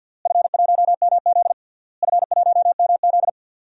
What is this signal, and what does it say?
Morse code, heard as a keyed tone, about 51 wpm, 700 Hz. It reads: F1MZ F1MZ
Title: DXCC-100 Call Signs - 4 Characters - Repeated 3x - 50wpm